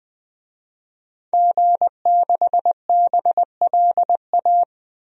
Morse code, heard as a keyed tone, about 20 wpm, 700 Hz. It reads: G6BLA
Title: DXCC-100 Call Signs - 5 Characters - Repeated 2x - 20wpm